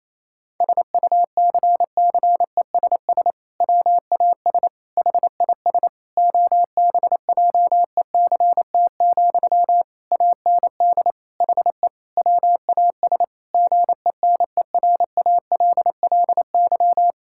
Morse code, heard as a keyed tone, about 28 wpm, 700 Hz. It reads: SUCCESS WAS HIS OBJECT, AND HE WAS GENERALLY
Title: Sentences from Top 1000 Words - Review Part 6 - 28wpm